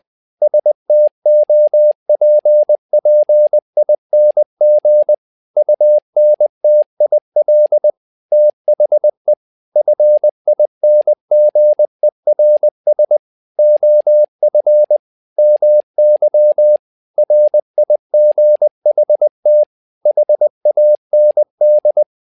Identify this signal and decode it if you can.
Morse code, heard as a keyed tone, about 20 wpm, 600 Hz. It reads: STOPPING UNTIL THE FINGERS OF MY RIGHT HAND
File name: learning center / tower trouble / tower trouble 20wpm